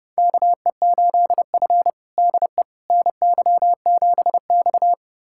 Morse code, heard as a keyed tone, about 30 wpm, 700 Hz. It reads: KE8F DE NY7X
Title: USA Callsign DE Callsign - 4 Characters - 30wpm